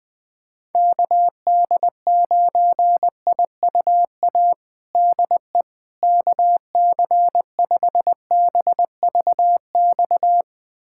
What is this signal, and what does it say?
Morse code, heard as a keyed tone, about 20 wpm, 700 Hz. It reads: KD9IUA DE KC5BVX
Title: USA Callsign DE Callsign 20wpm